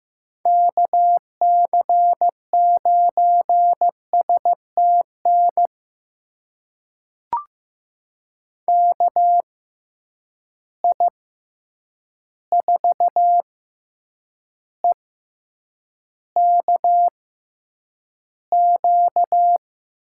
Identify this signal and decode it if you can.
Morse code, heard as a keyed tone, about 15 wpm, spaced as for 8 wpm, 700 Hz. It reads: KC9STN KI4EKQ
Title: USA Call Signs - 6 Characters - 6x Character Spacing - 15wpm